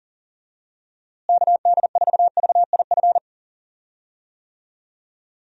Morse code, heard as a keyed tone, about 40 wpm, 700 Hz. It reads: KD4VIF
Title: USA Call Signs - 6 Characters - Repeated 3x - 40wpm